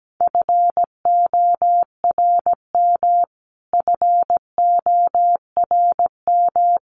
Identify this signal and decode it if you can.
Morse code, heard as a keyed tone, about 17 wpm, 700 Hz. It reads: FORM FORM
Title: Top 500 Words Repeated 17wpm